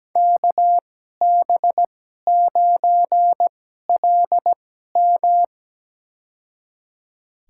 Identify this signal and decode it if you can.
Morse code, heard as a keyed tone, about 17 wpm, spaced as for 12 wpm, 700 Hz. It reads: KB9LM
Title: USA Call Signs - 5 Characters - 2x Character Spacing - 17wpm